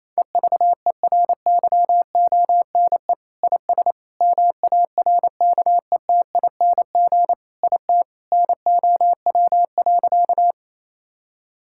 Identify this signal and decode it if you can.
Morse code, heard as a keyed tone, about 28 wpm, 700 Hz. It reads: EVERYONE IS MARKETING IT NOW.
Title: Sentences from Top 1000 Words - Review Part 5 - 28wpm